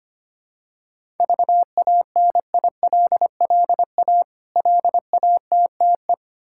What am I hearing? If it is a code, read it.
Morse code, heard as a keyed tone, about 25 wpm, 700 Hz. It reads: VANILLA LATTE
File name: practice / 25wpm / Difficult Words with Hints - 25wpm